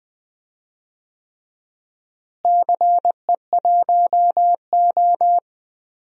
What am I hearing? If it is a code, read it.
Morse code, heard as a keyed tone, about 20 wpm, 700 Hz. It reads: CE1O
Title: DXCC-100 Call Signs - 4 Characters - Repeated 3x - 20wpm